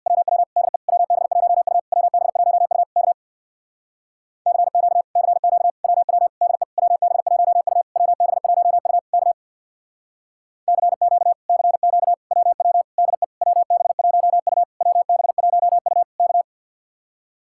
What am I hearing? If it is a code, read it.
Morse code, heard as a keyed tone, about 61 wpm, 700 Hz. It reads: WW DE WB1U WB1U K CQ CQ WW DE WB1U WB1U K CQ CQ WW DE WB1U WB1U K
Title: Speed-Racing Common QSO Phrases Encore - 50wpm